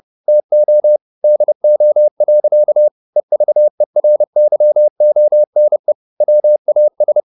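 Morse code, heard as a keyed tone, about 30 wpm, 600 Hz. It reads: TO DO. EVERYONE WAS
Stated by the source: Morse code at 30 wpm, the novel Never by Ken Follett in English, from CW med Gnister.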